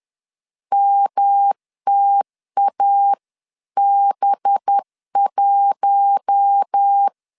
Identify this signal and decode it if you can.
Morse code, heard as a keyed tone, about 10 wpm, 800 Hz. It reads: MTAB1
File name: LEZIONI / 6 PKX0 / 6 PK 10wpm 2436